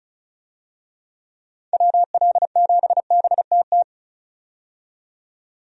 Morse code, heard as a keyed tone, about 35 wpm, 700 Hz. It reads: WL7BTT
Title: USA Call Signs - 6 Characters - 35wpm